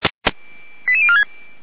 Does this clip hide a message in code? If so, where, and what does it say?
Yes: five-tone selective call (ZVEI) from 0.9 to 1.2 s, heard as quick runs of five tones, tones of about 70 ms. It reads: 88046